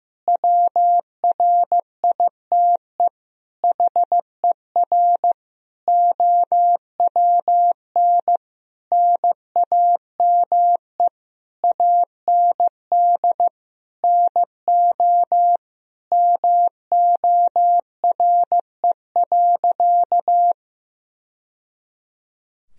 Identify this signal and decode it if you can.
Morse code, heard as a keyed tone, about 15 wpm, 700 Hz. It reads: WRITE HER OWN NAME AND NO MORE.
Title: Sentences from Top 1000 Words - Review Part 1 - 15wpm